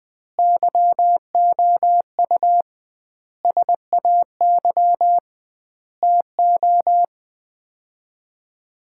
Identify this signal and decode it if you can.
Morse code, heard as a keyed tone, about 20 wpm, 700 Hz. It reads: YOU SAY TO
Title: 2x Word-Spacing - Sets of 5 Words - 20wpm